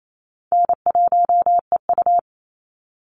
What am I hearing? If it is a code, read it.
Morse code, heard as a keyed tone, about 28 wpm, 700 Hz. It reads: N1EU
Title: NA SOTA Activator Callsigns 28wpm